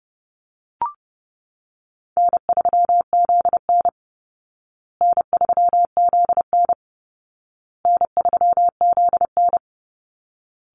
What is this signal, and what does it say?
Morse code, heard as a keyed tone, about 30 wpm, 700 Hz. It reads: N3ZN N3ZN N3ZN
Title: Easy Call Signs Repeated 3x - 30wpm